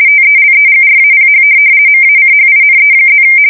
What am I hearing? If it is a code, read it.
RTTY, about 50 baud. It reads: TELEGRAPH MACHINE+
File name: fsk170-50B / teletype-corp